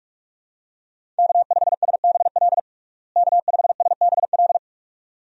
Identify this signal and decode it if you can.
Morse code, heard as a keyed tone, about 46 wpm, 700 Hz. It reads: K5SBL K5SBL
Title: USA Call Signs - 5 Characters - Repeated 2x - 45wpm